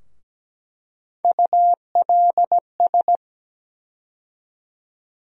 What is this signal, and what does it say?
Morse code, heard as a keyed tone, about 17 wpm, 700 Hz. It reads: ULS
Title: USA Call Sign Postfixes - 3 Letters 17wpm